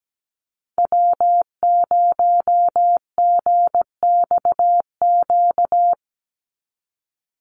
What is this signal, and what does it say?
Morse code, heard as a keyed tone, about 17 wpm, 700 Hz. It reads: W0GXQ
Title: Call Signs Repeated 3x - 17wpm